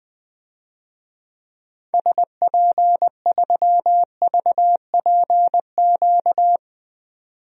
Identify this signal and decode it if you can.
Morse code, heard as a keyed tone, about 20 wpm, 700 Hz. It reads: SP3VPQ
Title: DXCC-100 Call Signs - 6 Characters - 20wpm